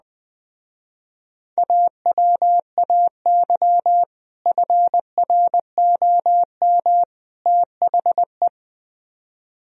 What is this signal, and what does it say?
Morse code, heard as a keyed tone, about 20 wpm, 700 Hz. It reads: AWAY FROM THE